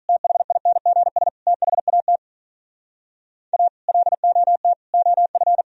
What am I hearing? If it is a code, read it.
Morse code, heard as a keyed tone, about 46 wpm, 700 Hz. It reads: THINGS THAT A LOT OF